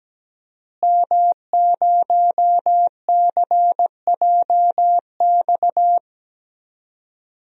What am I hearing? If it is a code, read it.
Morse code, heard as a keyed tone, about 17 wpm, 700 Hz. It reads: M0CJX
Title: DXCC-100 Call Signs - 5 Characters - Repeated 3x - 17wpm